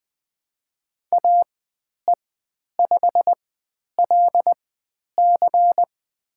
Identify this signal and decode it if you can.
Morse code, heard as a keyed tone, about 20 wpm, spaced as for 10 wpm, 700 Hz. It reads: AE5LC
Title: 102 Lesson 17g - Letter u - Call Signs 20wpm